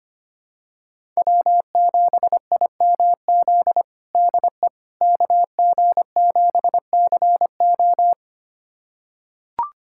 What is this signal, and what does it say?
Morse code, heard as a keyed tone, about 25 wpm, 700 Hz. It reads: W7IMZ DE KG7CO
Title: USA Callsign DE Callsign - 5 Characters - 25wpm